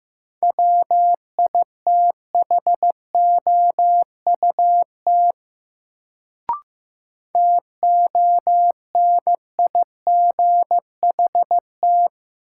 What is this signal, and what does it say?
Morse code, heard as a keyed tone, about 16 wpm, 700 Hz. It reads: WITHOUT TONIGHT
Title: Sentences from Top 600 Words - New Word Drill - 15wpm